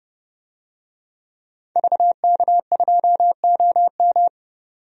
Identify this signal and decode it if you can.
Morse code, heard as a keyed tone, about 32 wpm, 700 Hz. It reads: VK2OM
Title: DXCC-100 Call Signs - 5 Characters - 30wpm